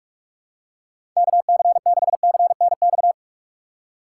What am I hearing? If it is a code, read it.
Morse code, heard as a keyed tone, about 46 wpm, 700 Hz. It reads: KC6CNX
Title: USA Call Signs - 6 Characters - 3x Character Spacing - 45wpm